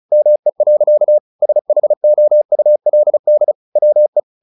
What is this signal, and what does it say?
Morse code, heard as a keyed tone, about 34 wpm, 600 Hz. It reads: ME. SHOULD WE
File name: the old man and the sea / the old man and the sea 1 35wpm